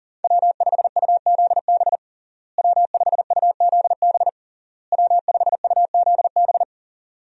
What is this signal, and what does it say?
Morse code, heard as a keyed tone, about 37 wpm, 700 Hz. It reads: W5UZB W5UZB W5UZB
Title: USA Call Signs - 5 Characters - Repeated 3x - 40wpm